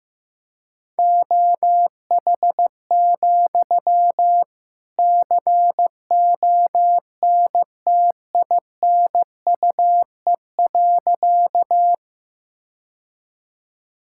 Morse code, heard as a keyed tone, about 16 wpm, 700 Hz. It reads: OH, CONTINUE.
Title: Sentences from Top 1000 Words - Review Part 6 - 15wpm